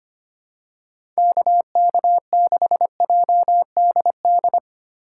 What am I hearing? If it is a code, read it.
Morse code, heard as a keyed tone, about 25 wpm, 700 Hz. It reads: KK6JDD
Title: USA Call Signs - 6 Characters - 4x Character Spacing - 25wpm